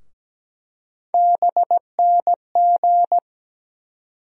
Morse code, heard as a keyed tone, about 17 wpm, 700 Hz. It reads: BNG
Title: USA Call Sign Postfixes - 3 Letters 17wpm